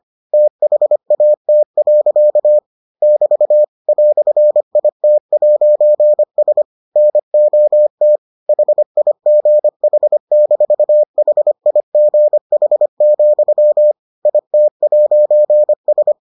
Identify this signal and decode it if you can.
Morse code, heard as a keyed tone, about 25 wpm, 600 Hz. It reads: THAT. <BT> "IT'S NOT HIGH-HIGH, IT'S